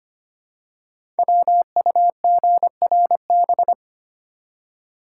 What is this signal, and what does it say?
Morse code, heard as a keyed tone, about 25 wpm, 700 Hz. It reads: WUGRB
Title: Code Groups - 5 Letters - 25wpm